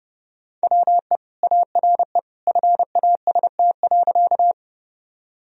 Morse code, heard as a keyed tone, about 30 wpm, 700 Hz. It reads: WE ARE FAST.
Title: Sentences from Top 1000 Words - Review Part 2 - 30wpm